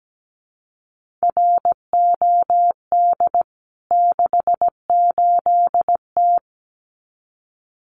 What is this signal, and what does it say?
Morse code, heard as a keyed tone, about 17 wpm, 700 Hz. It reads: ROD 68T
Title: CWT Exchange 17wpm